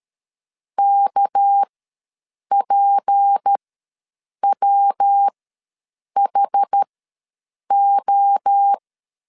Morse code, heard as a keyed tone, about 13 wpm, spaced as for 7 wpm, 800 Hz. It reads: KPWHO
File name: LEZIONI / 9 28 ws 12wpm 3354